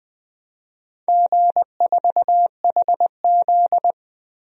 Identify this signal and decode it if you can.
Morse code, heard as a keyed tone, about 20 wpm, 700 Hz. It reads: G4HZ
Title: DXCC-100 Call Signs - 4 Characters - 20wpm